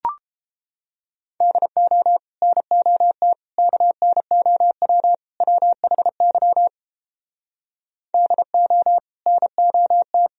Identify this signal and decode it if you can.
Morse code, heard as a keyed tone, about 29 wpm, 700 Hz. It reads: DO NOT KNOW WHY DO NOT